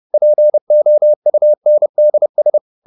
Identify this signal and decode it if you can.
Morse code, heard as a keyed tone, about 30 wpm, 600 Hz. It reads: POUNDS
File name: the old man and the sea / the old man and the sea 1 30wpm